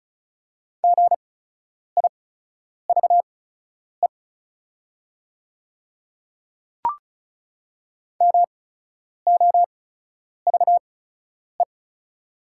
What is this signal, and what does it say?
Morse code, heard as a keyed tone, about 36 wpm, spaced as for 10 wpm, 700 Hz. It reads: GIVE MOVE
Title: 8x Character Spacing - Top 200 Words - 35wpm